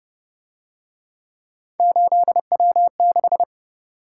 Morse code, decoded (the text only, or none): 8W6